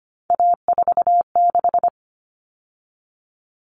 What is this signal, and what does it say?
Morse code, heard as a keyed tone, about 25 wpm, 700 Hz. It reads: A46
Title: Common Cut Numbers - 3-Digit 25wpm